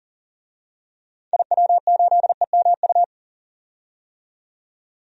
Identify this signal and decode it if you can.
Morse code, heard as a keyed tone, about 40 wpm, 700 Hz. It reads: IW8EMU
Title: DXCC-100 Call Signs - 6 Characters - 40wpm